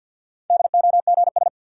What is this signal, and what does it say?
Morse code, heard as a keyed tone, about 51 wpm, 700 Hz. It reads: DOGS
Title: Sentences from Top 1600 Words - New Word Drill - 50wpm